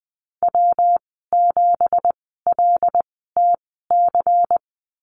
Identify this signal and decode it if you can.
Morse code, heard as a keyed tone, about 20 wpm, spaced as for 14 wpm, 700 Hz. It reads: W7LTC
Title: 186 Lesson 30g - Letter b - Call Signs 20wpm